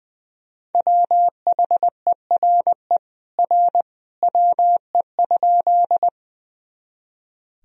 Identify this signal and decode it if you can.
Morse code, heard as a keyed tone, about 20 wpm, 700 Hz. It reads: WHERE R WE?
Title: Sentences from Top 200 Words and Abbreviations 20wpm